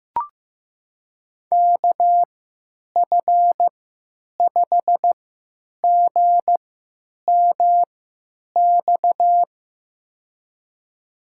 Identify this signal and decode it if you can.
Morse code, heard as a keyed tone, about 15 wpm, spaced as for 9 wpm, 700 Hz. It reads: KF5GMX
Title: USA Call Signs - 6 Characters - 3x Character Spacing - 15wpm